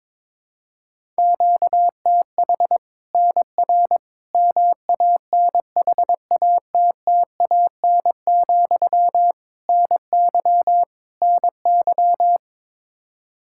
Morse code, heard as a keyed tone, about 22 wpm, 700 Hz. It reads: QTH NR MANHATTAN, NY NY
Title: Practice QSO - Exchange 1 - Beginning - 22wpm